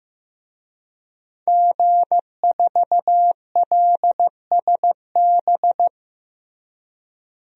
Morse code, heard as a keyed tone, about 15 wpm, 700 Hz. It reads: G4LSB
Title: DXCC-100 Call Signs - 5 Characters - Repeated 3x - 15wpm